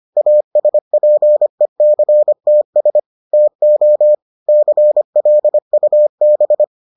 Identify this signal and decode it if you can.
Morse code, heard as a keyed tone, about 25 wpm, 600 Hz. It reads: ASPECTS TO CLUB